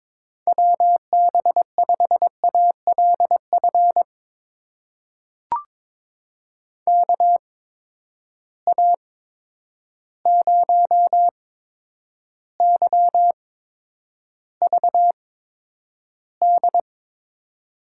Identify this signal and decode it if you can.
Morse code, heard as a keyed tone, about 22 wpm, spaced as for 9 wpm, 700 Hz. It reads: WB5ALF KA0YVD
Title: USA Call Signs - 6 Characters - 8x Character Spacing - 22wpm